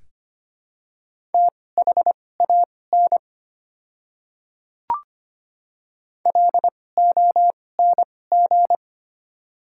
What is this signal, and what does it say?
Morse code, heard as a keyed tone, about 25 wpm, spaced as for 18 wpm, 700 Hz. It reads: THAN LONG
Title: 2x Character Spacing - Top 100 Words - 25wpm